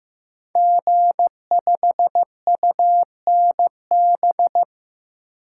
Morse code, heard as a keyed tone, about 15 wpm, 700 Hz. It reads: G5UNB
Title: Code Groups - 5 Letter:Numbers - 15wpm